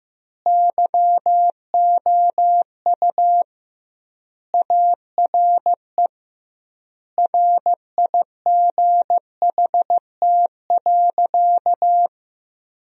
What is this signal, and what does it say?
Morse code, heard as a keyed tone, about 15 wpm, 700 Hz. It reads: YOU ARE RIGHT.
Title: 2x Word-Spacing - Sentences from Top 300 Words - 15wpm